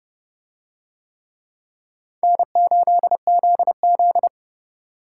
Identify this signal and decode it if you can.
Morse code, heard as a keyed tone, about 28 wpm, 700 Hz. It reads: N8ZZ